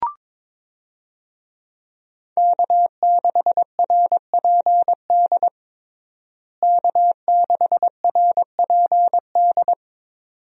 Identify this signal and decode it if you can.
Morse code, heard as a keyed tone, about 22 wpm, 700 Hz. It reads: K6RPD K6RPD
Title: USA Call Signs - 5 Characters - Repeated 2x - 22wpm